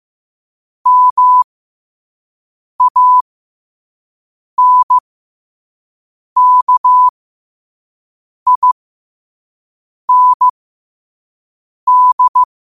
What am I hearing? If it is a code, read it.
Morse code, heard as a keyed tone, about 15 wpm, spaced as for 5 wpm, 1000 Hz. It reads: MANKIND
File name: qotdfiles / 5wpm